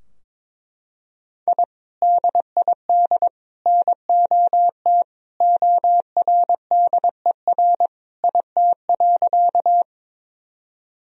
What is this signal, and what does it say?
Morse code, heard as a keyed tone, about 22 wpm, 700 Hz. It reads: I DID NOT ORDER IT.
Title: Sentences from Top 1000 Words - Review Part 2 - 22wpm